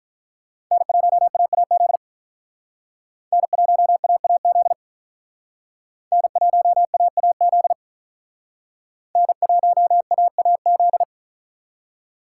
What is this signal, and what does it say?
Morse code, heard as a keyed tone, about 44 wpm, 700 Hz. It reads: N1AAZ N1AAZ N1AAZ N1AAZ